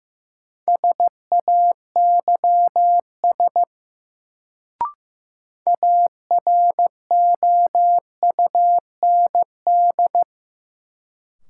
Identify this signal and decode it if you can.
Morse code, heard as a keyed tone, about 15 wpm, 700 Hz. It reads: SAYS AROUND
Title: Sets of 2 Words - New Word Drill - 15wpm